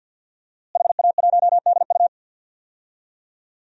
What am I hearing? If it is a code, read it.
Morse code, heard as a keyed tone, about 52 wpm, 700 Hz. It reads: HA1DU
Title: DXCC-100 Call Signs - 5 Characters - Repeated 2x - 50wpm